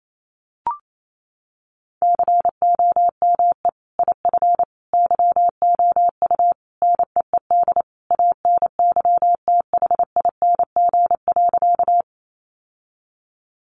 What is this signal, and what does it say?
Morse code, heard as a keyed tone, about 28 wpm, 700 Hz. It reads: COME IF YOU NEED ANYTHING.
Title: Sentences from Top 1000 Words 28wpm